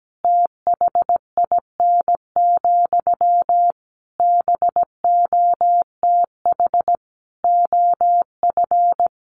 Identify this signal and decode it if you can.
Morse code, heard as a keyed tone, about 17 wpm, 700 Hz. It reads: THIN, BOTH OF
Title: Sentences from Top 1200 Words 17wpm